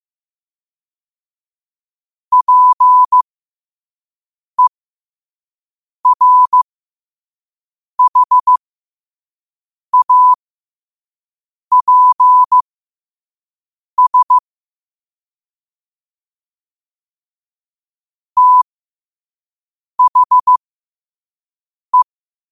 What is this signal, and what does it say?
Morse code, heard as a keyed tone, about 15 wpm, spaced as for 5 wpm, 1000 Hz. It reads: PERHAPS THE
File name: qotdfiles / 5wpm